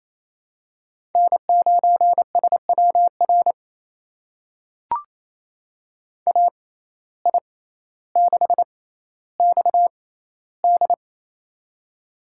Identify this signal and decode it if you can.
Morse code, heard as a keyed tone, about 28 wpm, spaced as for 14 wpm, 700 Hz. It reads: N9SWR AI6XD
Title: USA Call Signs - 5 Characters - 6x Character Spacing - 28wpm